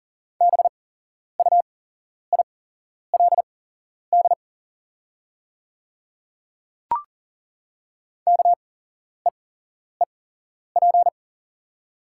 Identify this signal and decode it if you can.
Morse code, heard as a keyed tone, about 40 wpm, spaced as for 11 wpm, 700 Hz. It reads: BUILD KEEP
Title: 8x Character Spacing - Top 300 Words - 40wpm